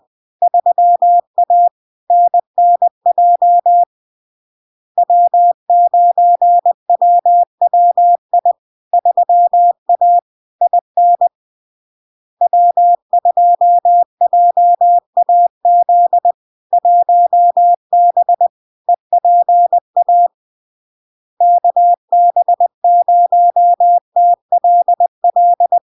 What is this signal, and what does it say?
Morse code, heard as a keyed tone, about 20 wpm, 700 Hz. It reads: 3A NNJ W9WWI 3A IN W2JAZ 1B EPA KB0TLL
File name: LARK-FD-2022-Log-in-CW-20wpm-700-Hz